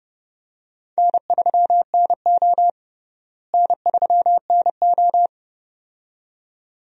Morse code, heard as a keyed tone, about 30 wpm, 700 Hz. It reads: N3NO N3NO
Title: USA Call Signs - 4 Characters - Repeated 2x - 30wpm